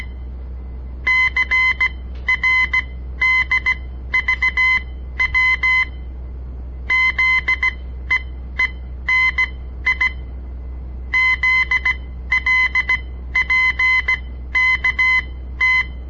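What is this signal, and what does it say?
Morse code, heard as a keyed tone, about 17 wpm, spaced as for 12 wpm, 2050 Hz. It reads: CRDVW ZEENI ZLPKT
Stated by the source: On-air Morse code practice broadcast by a southern Montana amateur radio association; all code, no read back.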